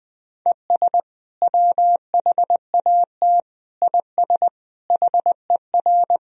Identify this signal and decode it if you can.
Morse code, heard as a keyed tone, about 20 wpm, 700 Hz. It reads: ES WHAT IS HER
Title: Sentences from Top 200 Words and Abbreviations 20wpm